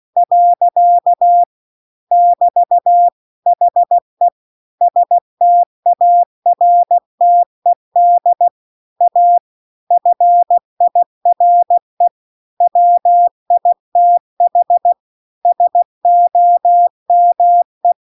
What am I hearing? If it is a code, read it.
Morse code, heard as a keyed tone, about 16 wpm, 700 Hz. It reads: . <BT> HE STARTED A FIRE WITH SOME